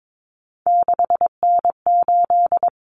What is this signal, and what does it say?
Morse code, heard as a keyed tone, about 22 wpm, 700 Hz. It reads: 6N8
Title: Common Cut Numbers - 3-Digit 22wpm